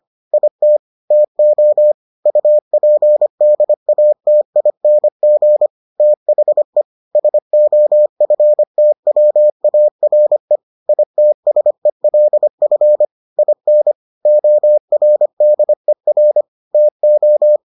Morse code, heard as a keyed tone, about 25 wpm, 600 Hz. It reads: IT TO UPDATING THE SOFTWARE ITSELF IN ORDER TO